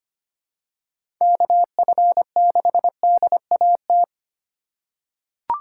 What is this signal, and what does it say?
Morse code, heard as a keyed tone, about 25 wpm, 700 Hz. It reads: KF6DAT
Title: USA Call Signs - 6 Characters - 3x Character Spacing - 25wpm